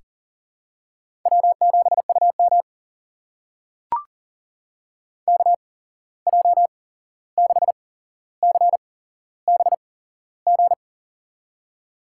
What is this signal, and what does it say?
Morse code, heard as a keyed tone, about 40 wpm, spaced as for 15 wpm, 700 Hz. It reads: W7UM KJ6CBG
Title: USA Call Signs - 8x Character Spacing - 40wpm